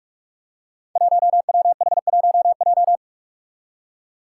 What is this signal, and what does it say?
Morse code, heard as a keyed tone, about 44 wpm, 700 Hz. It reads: 1WH1J
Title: Five Challenging Letter-Number Combinations 45wpm